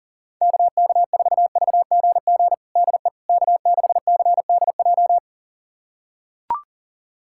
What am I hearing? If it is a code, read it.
Morse code, heard as a keyed tone, about 40 wpm, 700 Hz. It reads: KK4VGG DE K6CDJ
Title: USA Callsign DE Callsign 40wpm